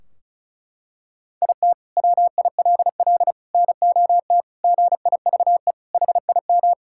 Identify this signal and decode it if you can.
Morse code, heard as a keyed tone, about 36 wpm, 700 Hz. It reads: IT WILL NOT GIVE HIM
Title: Sentences from Top 1600 Words 35wpm